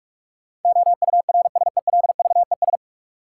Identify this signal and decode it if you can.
Morse code, heard as a keyed tone, about 46 wpm, 700 Hz. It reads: OURSELVES
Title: Sentences from Top 1600 Words - New Word Drill - 45wpm